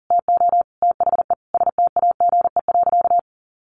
Morse code, heard as a keyed tone, about 40 wpm, 700 Hz. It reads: TO THE STAGE.
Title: Sentences from Top 1200 Words 40wpm